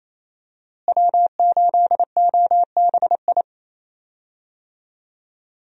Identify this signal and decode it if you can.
Morse code, heard as a keyed tone, about 28 wpm, 700 Hz. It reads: W8OBI
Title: USA Call Signs - 5 Characters - Repeated 2x - 28wpm